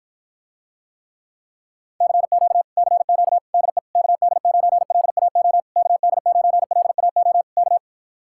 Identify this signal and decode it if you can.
Morse code, heard as a keyed tone, about 53 wpm, 700 Hz. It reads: CQ CQ DE KD9LAO KD9LAO K